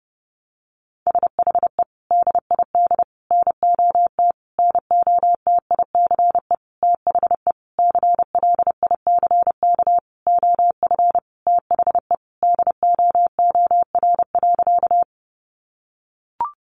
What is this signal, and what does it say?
Morse code, heard as a keyed tone, about 30 wpm, 700 Hz. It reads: SHE DID NOT NOTICE THE CLICK OF THE DOOR.
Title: Sentences from Top 1100 Words 30wpm